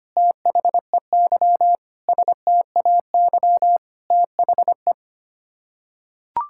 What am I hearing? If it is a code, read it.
Morse code, heard as a keyed tone, about 25 wpm, 700 Hz. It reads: THEY STAY THE
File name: Sets of 9 Words - 25wpm